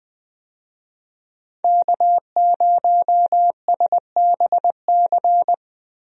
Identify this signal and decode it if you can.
Morse code, heard as a keyed tone, about 20 wpm, 700 Hz. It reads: K0SBC
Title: USA Call Signs - 5 Characters - Repeated 3x - 20wpm